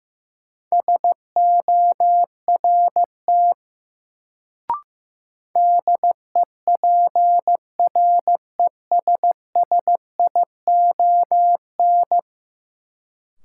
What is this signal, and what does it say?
Morse code, heard as a keyed tone, about 15 wpm, 700 Hz. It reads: SORT DEPRESSION
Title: Sentences from Top 1300 Words - New Word Drill Alt - 15wpm